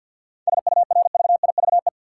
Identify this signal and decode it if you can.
Morse code, heard as a keyed tone, about 52 wpm, 700 Hz. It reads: SURVIVE
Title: Sentences from Top 1300 Words - New Word Drill - 50wpm